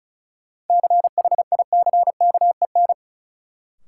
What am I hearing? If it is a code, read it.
Morse code, heard as a keyed tone, about 36 wpm, 700 Hz. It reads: CHICKEN
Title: Sentences from Top 1000 Words - Review Part 5 - New Word Drill - 35wpm